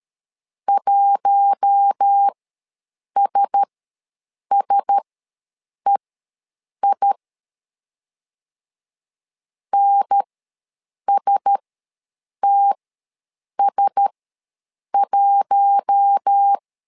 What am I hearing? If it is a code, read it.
Morse code, heard as a keyed tone, about 13 wpm, spaced as for 7 wpm, 800 Hz. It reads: 1SSEI NSTS1
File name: LEZIONI / 4 Q1S ws 12wpm 1827